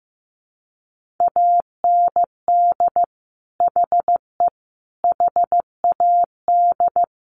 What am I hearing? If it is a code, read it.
Morse code, heard as a keyed tone, about 15 wpm, 700 Hz. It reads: AND HE HAD
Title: Sentences from Top 1400 Words 15wpm